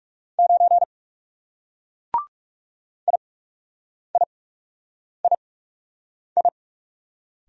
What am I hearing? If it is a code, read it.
Morse code, heard as a keyed tone, about 38 wpm, spaced as for 8 wpm, 700 Hz. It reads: 9 IIII